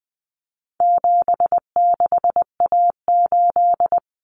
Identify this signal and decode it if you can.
Morse code, heard as a keyed tone, about 20 wpm, 700 Hz. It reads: 76A8
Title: Common Cut Numbers - 4-Digit 20wpm